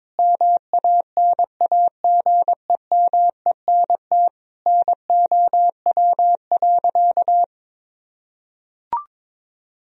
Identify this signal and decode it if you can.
Morse code, heard as a keyed tone, about 22 wpm, 700 Hz. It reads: MANAGEMENT NOW.
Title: Sentences from Top 1000 Words - Review Part 4 - 22wpm